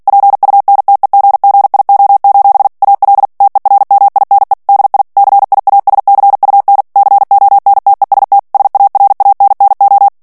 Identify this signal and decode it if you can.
Morse code, heard as a keyed tone, about 48 wpm, 800 Hz. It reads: PUNTEGGIO: AL TERMINE DI CIASCUN CONTEST SARANNO
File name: ITALIEN texte clair 2 vitesse 45 mots minute